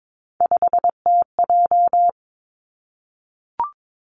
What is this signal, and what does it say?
Morse code, heard as a keyed tone, about 22 wpm, 700 Hz. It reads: 5TJ
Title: Three Challenging Letter-Number Combinations 22wpm